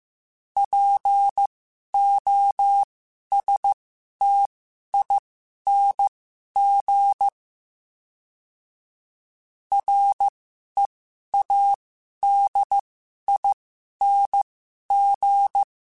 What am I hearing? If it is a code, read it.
Morse code, heard as a keyed tone, about 15 wpm, spaced as for 11 wpm, 800 Hz. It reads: POSTING READING
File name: Special INGs-mp3s / Special INGs-mp3s / Sngl Ing @10